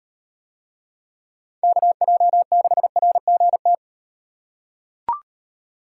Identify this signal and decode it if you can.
Morse code, heard as a keyed tone, about 35 wpm, 700 Hz. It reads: KJ6RGT